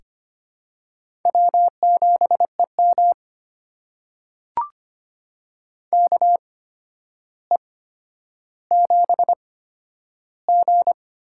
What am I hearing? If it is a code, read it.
Morse code, heard as a keyed tone, about 25 wpm, spaced as for 11 wpm, 700 Hz. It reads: W7EM KE7G
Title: USA Call Signs - 8x Character Spacing - 25wpm